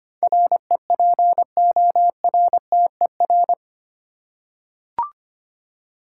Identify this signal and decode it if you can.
Morse code, heard as a keyed tone, about 25 wpm, 700 Hz. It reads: REPORTER
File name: 8 Letter Words - First 3 Letters Given - 25wpm